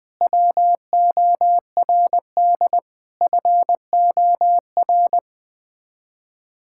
Morse code, heard as a keyed tone, about 20 wpm, 700 Hz. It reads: WORD FOR
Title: Sets of 8 Words - 20wpm